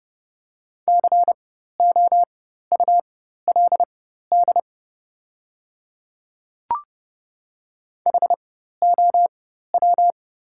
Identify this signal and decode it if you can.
Morse code, heard as a keyed tone, about 30 wpm, spaced as for 14 wpm, 700 Hz. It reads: COULD HOW
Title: 4x Character Spacing - Top 200 Words - 30wpm